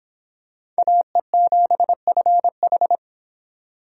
Morse code, heard as a keyed tone, about 25 wpm, 700 Hz. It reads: AE7FH